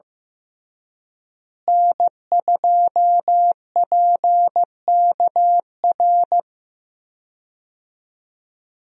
Morse code, heard as a keyed tone, about 15 wpm, 700 Hz. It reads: N2PKR
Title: USA Call Signs - 5 Characters - Repeated 3x - 15wpm